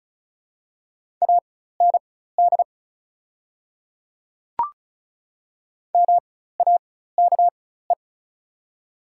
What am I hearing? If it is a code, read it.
Morse code, heard as a keyed tone, about 34 wpm, spaced as for 16 wpm, 700 Hz. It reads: AND MAKE
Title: 4x Character Spacing - Top 100 Words - 35wpm